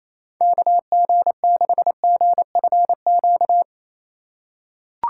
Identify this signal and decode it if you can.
Morse code, heard as a keyed tone, about 28 wpm, 700 Hz. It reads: KG6GFQ
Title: USA Call Signs - 6 Characters - 2x Character Spacing - 28wpm